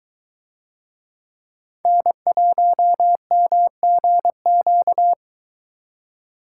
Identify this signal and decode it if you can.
Morse code, heard as a keyed tone, about 22 wpm, 700 Hz. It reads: N1MGQ